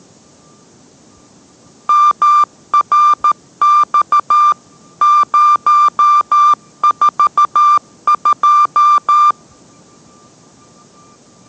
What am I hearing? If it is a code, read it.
Morse code, heard as a keyed tone, about 13 wpm, 1190 Hz. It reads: MRX 042